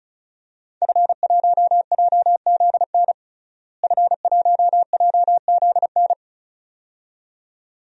Morse code, heard as a keyed tone, about 36 wpm, 700 Hz. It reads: F1JZN F1JZN
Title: DXCC-100 Call Signs - 5 Characters - Repeated 2x - 35wpm